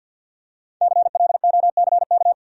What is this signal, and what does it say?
Morse code, heard as a keyed tone, about 52 wpm, 700 Hz. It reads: CLOCK
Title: Five Letter Words - 50wpm